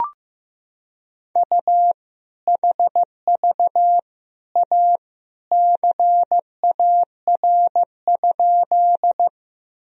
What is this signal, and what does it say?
Morse code, heard as a keyed tone, about 15 wpm, 700 Hz. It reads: U HV A CAR?
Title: Sentences from Top 300 Words and Abbreviations 15wpm